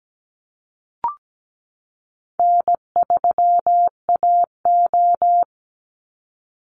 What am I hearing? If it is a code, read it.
Morse code, heard as a keyed tone, about 17 wpm, 700 Hz. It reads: N3AO
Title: Easy Call Signs Repeated 3x - 17wpm